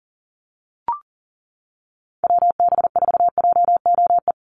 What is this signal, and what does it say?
Morse code, heard as a keyed tone, about 40 wpm, 700 Hz. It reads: WB4JOE
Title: USA Call Signs 40wpm